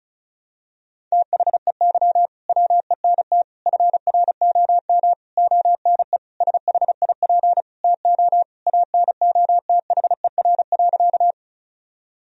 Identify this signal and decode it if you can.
Morse code, heard as a keyed tone, about 36 wpm, 700 Hz. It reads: THEY WENT FROM ONE SHIP TO ANOTHER.
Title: Sentences from Top 1000 Words - Review Part 2 - 35wpm